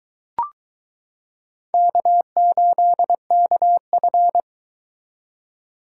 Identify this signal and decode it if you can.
Morse code, heard as a keyed tone, about 22 wpm, 700 Hz. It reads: K8KF